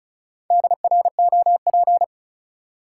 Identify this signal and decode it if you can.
Morse code, heard as a keyed tone, about 34 wpm, 700 Hz. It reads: DROP
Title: Sentences from Top 1400 Words - New Word Drill - 35wpm